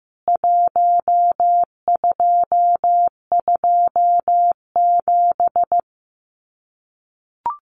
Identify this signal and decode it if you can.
Morse code, heard as a keyed tone, about 15 wpm, 700 Hz. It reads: 1227
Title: CWT Exchange 15wpm